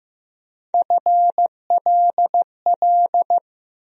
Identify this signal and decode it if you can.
Morse code, heard as a keyed tone, about 15 wpm, 700 Hz. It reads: FLL
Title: USA Call Sign Postfixes - 3 Letters 15wpm